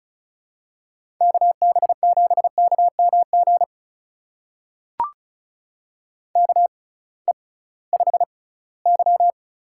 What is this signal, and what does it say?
Morse code, heard as a keyed tone, about 34 wpm, spaced as for 20 wpm, 700 Hz. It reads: KB7KMG KE5Y
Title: USA Call Signs - 6 Characters - 6x Character Spacing - 35wpm